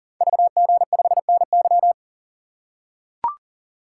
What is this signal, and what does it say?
Morse code, heard as a keyed tone, about 40 wpm, 690 Hz. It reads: VG5NY
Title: Code Groups - 5 Letter:Numbers - 40wpm